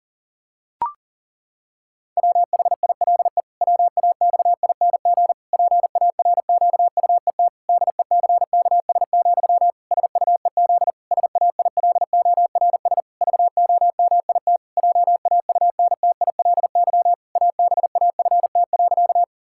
Morse code, heard as a keyed tone, about 40 wpm, 700 Hz. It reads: WHILE WAXING PARQUET DECKS, SUEZ SAILORS VOMIT JAUNTILY ABAFT.
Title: Pangram Challenge v2 - 40wpm